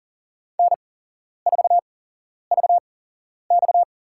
Morse code, heard as a keyed tone, about 40 wpm, spaced as for 11 wpm, 700 Hz. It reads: N4VX
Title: USA Call Signs - 4 Characters - 8x Character Spacing - 40wpm